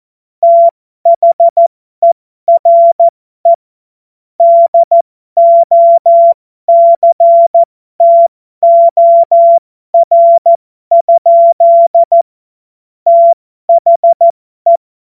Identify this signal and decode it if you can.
Morse code, heard as a keyed tone, about 14 wpm, spaced as for 12 wpm, 680 Hz. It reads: THERE DOCTOR? THE